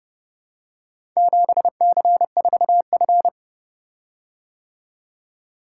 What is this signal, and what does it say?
Morse code, heard as a keyed tone, about 29 wpm, 700 Hz. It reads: 7C4F